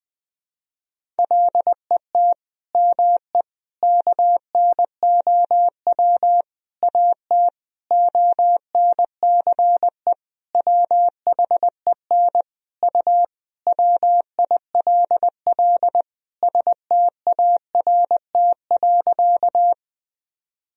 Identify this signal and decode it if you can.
Morse code, heard as a keyed tone, about 21 wpm, 700 Hz. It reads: LET ME KNOW AT ONCE WHEN U WILL START.
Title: Sentences from Top 300 Words and Abbreviations 20wpm